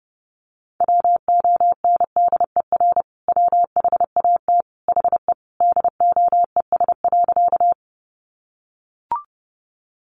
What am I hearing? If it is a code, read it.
Morse code, heard as a keyed tone, about 28 wpm, 700 Hz. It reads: WONDER WHAT HE DOES.
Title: Sentences from Top 500 Words 30wpm